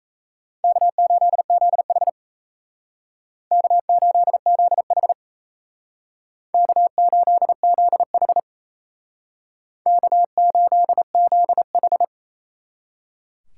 Morse code, heard as a keyed tone, about 34 wpm, 700 Hz. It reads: K8ZH K8ZH K8ZH K8ZH
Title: Speed Racing - USA 1x2 Call Signs 28wpm